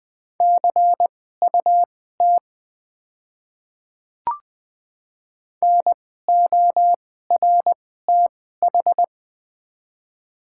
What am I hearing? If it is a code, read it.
Morse code, heard as a keyed tone, about 20 wpm, spaced as for 14 wpm, 690 Hz. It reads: CUT NORTH
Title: 2x Character Spacing - Top 500 Words - 20wpm